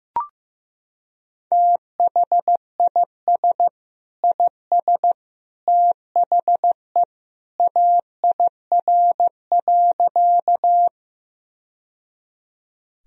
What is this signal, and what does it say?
Morse code, heard as a keyed tone, about 15 wpm, 700 Hz. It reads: THIS IS THE AIR.
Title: Sentences from Top 1000 Words - Review Part 1 - 15wpm